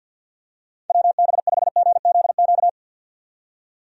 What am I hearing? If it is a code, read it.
Morse code, heard as a keyed tone, about 47 wpm, 700 Hz. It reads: WB5GZQ
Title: USA Call Signs - 6 Characters - 50wpm